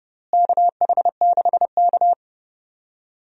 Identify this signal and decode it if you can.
Morse code, heard as a keyed tone, about 30 wpm, 700 Hz. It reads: KH6K
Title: DXCC-100 Call Signs - 4 Characters - Repeated 3x - 30wpm